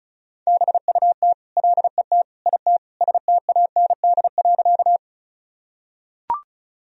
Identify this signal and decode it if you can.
Morse code, heard as a keyed tone, about 36 wpm, 700 Hz. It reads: BUT LET IT STAND.
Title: Sentences from Top 1000 Words - Review Part 1 - 35wpm